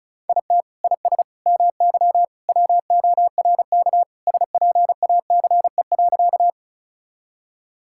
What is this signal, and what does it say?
Morse code, heard as a keyed tone, about 36 wpm, 700 Hz. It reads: IT IS MY WORK SPACE.
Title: Sentences from Top 1000 Words - Review Part 5 - 35wpm